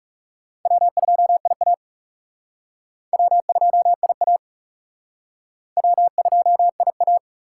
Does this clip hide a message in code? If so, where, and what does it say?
Yes: Morse code from 0.7 to 7.2 s, heard as a keyed tone, about 40 wpm, 700 Hz. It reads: W2IA W2IA W2IA